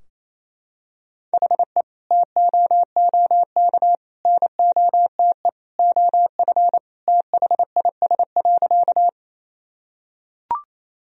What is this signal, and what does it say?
Morse code, heard as a keyed tone, about 27 wpm, 700 Hz. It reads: HE TOOK NOTE OF THIS.
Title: Sentences from Top 1000 Words - Review Part 2 - 28wpm